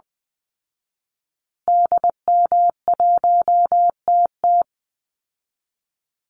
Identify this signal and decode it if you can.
Morse code, heard as a keyed tone, about 20 wpm, 700 Hz. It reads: DM1TT
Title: Call Signs Repeated 3x - 20wpm